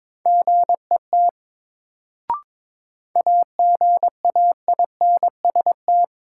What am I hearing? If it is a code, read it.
Morse code, heard as a keyed tone, about 22 wpm, 700 Hz. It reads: GET AGAINST
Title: Sets of 2 Words - New Word Drill - 22wpm